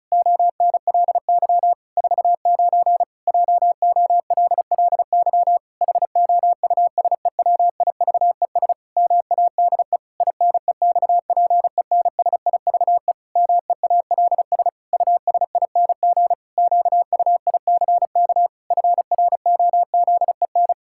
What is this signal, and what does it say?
Morse code, heard as a keyed tone, about 34 wpm, 700 Hz. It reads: ONLY 49 JOLLY HOUSEWIVES MADE INEXPENSIVE MEALS USING QUICK FROZEN
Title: Pangram Challenge v2 - 35wpm